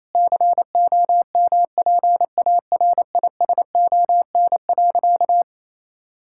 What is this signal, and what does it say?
Morse code, heard as a keyed tone, about 28 wpm, 700 Hz. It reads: COMPARISON.
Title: Sentences from Top 1600 Words 28wpm